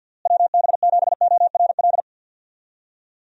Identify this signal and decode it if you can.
Morse code, heard as a keyed tone, about 45 wpm, 700 Hz. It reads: WB7ORL
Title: USA Call Signs - 6 Characters - Repeated 3x - 50wpm